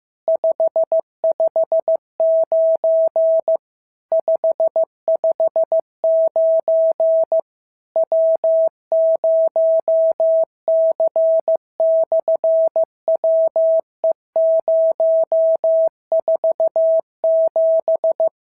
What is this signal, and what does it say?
Morse code, heard as a keyed tone, about 15 wpm, 650 Hz. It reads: 559 559 W0C/WE047
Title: SOTA QSO Practice 15wpm